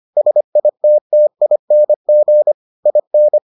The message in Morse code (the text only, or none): SITTING IN